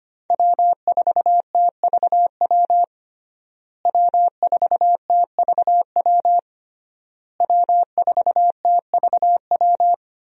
Morse code, heard as a keyed tone, about 25 wpm, 700 Hz. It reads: W4TVW W4TVW W4TVW
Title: USA Call Signs - 5 Characters - Repeated 3x - 25wpm